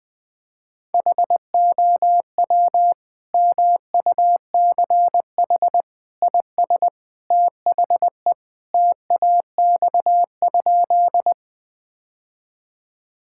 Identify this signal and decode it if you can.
Morse code, heard as a keyed tone, about 20 wpm, 700 Hz. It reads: HOW MUCH IS THE TAX?
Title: Sentences from Top 1000 Words - Review Part 5 - 20wpm